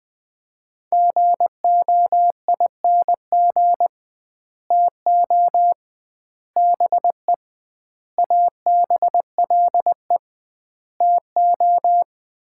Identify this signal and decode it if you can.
Morse code, heard as a keyed tone, about 21 wpm, 700 Hz. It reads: GOING TO BE ABLE TO
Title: 2x Word-Spacing - Sets of 5 Words - 20wpm